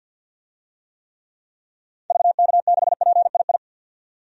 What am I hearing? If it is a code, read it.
Morse code, heard as a keyed tone, about 48 wpm, 700 Hz. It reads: VK6PII
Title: DXCC-100 Call Signs - 6 Characters - 50wpm